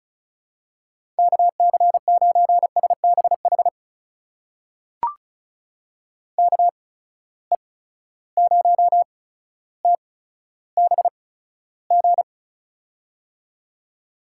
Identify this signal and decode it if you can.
Morse code, heard as a keyed tone, about 34 wpm, spaced as for 15 wpm, 700 Hz. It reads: KC9SBH KE0TBG
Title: USA Call Signs - 6 Characters - 8x Character Spacing - 35wpm